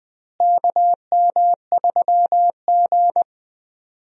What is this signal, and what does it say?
Morse code, heard as a keyed tone, about 21 wpm, 700 Hz. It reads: KM3G
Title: USA Call Signs - 4 Characters - Repeated 2x - 20wpm